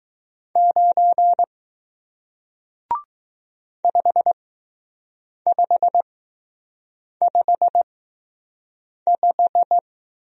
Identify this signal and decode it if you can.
Morse code, heard as a keyed tone, about 20 wpm, spaced as for 6 wpm, 700 Hz. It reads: 9 5555